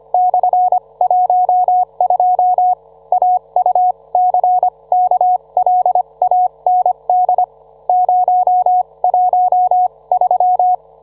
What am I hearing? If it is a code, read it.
Morse code, heard as a keyed tone, about 25 wpm, spaced as for 21 wpm, 730 Hz, with radio noise. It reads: /12 AUCKLAND 013